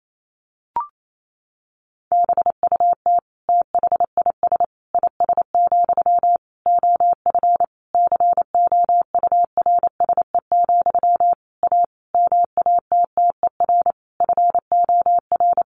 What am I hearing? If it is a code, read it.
Morse code, heard as a keyed tone, about 28 wpm, 700 Hz. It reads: BUT THIS IS, OF COURSE, A MATTER FOR
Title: Sentences from Top 1500 Words 28wpm